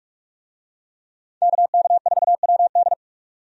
Morse code, heard as a keyed tone, about 46 wpm, 700 Hz. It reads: KK4WD
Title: USA Call Signs - 5 Characters - 45wpm